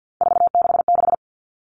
Morse code, heard as a keyed tone, about 50 wpm, 700 Hz. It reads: 466
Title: All Cut Numbers - 3-Digit 50wpm